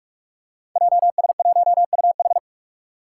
Morse code, heard as a keyed tone, about 44 wpm, 700 Hz. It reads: JS1UH
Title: Five Challenging Letter-Number Combinations 45wpm